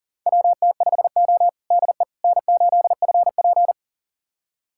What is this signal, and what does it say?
Morse code, heard as a keyed tone, about 40 wpm, 700 Hz. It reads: WT5O DE N8FP
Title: USA Callsign DE Callsign - 4 Characters - 40wpm